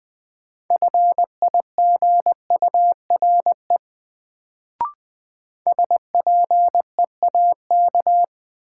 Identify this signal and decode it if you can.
Morse code, heard as a keyed tone, about 20 wpm, 700 Hz. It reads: FIGURE SPEAK
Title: Sentences from Top 1000 Words - Review Part 2 - New Word Drill - 20wpm